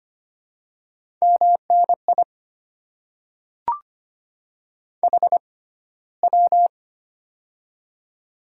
Morse code, heard as a keyed tone, about 25 wpm, spaced as for 15 wpm, 700 Hz. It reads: MNI HW
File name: Top 90 QSO Elements - 6x Character Spacing - 25wpm